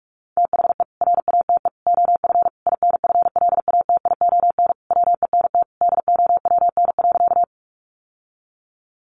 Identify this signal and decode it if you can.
Morse code, heard as a keyed tone, about 44 wpm, 700 Hz. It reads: THE RATE OF INFLATION WENT DOWN.
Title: Sentences from Top 1100 Words 45wpm